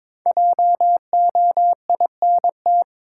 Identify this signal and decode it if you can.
Morse code, heard as a keyed tone, about 21 wpm, 700 Hz. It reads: JOINT
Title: Sentences from Top 1300 Words - New Word Drill Alt - 22wpm